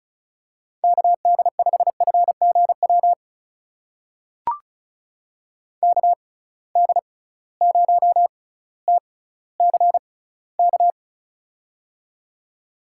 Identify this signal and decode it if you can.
Morse code, heard as a keyed tone, about 36 wpm, spaced as for 18 wpm, 700 Hz. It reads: KD5FGW KD0TCK
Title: USA Call Signs - 6 Characters - 6x Character Spacing - 35wpm